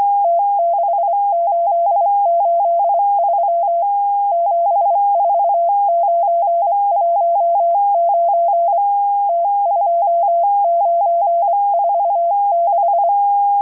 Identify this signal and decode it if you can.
Morse code, heard as a keyed tone, about 25 wpm, spaced as for 22 wpm, 680 Hz. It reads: T6883 74919 T2946